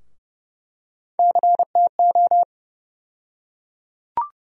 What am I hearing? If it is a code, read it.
Morse code, heard as a keyed tone, about 30 wpm, 700 Hz. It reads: CTO
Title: USA Call Sign Postfixes - 3 Letters 30wpm